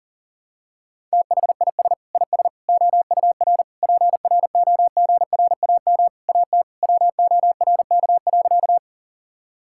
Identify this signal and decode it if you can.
Morse code, heard as a keyed tone, about 40 wpm, 700 Hz. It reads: THIS IS OUR PROGRAM AT WORK.
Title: Sentences from Top 1000 Words - Review Part 5 - 40wpm